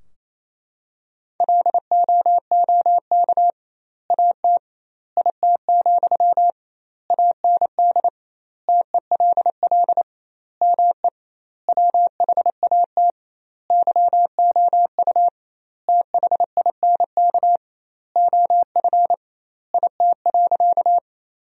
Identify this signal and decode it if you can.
Morse code, heard as a keyed tone, about 27 wpm, 700 Hz. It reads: LOOK AT IT, AND TELL ME WHAT YOU THINK OF IT.
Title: 2x Word-Spacing - Sentences from Top 200 Words - 28wpm